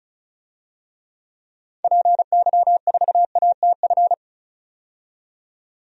Morse code, heard as a keyed tone, about 34 wpm, 700 Hz. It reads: PY4ATF
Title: DXCC-100 Call Signs - 6 Characters - 35wpm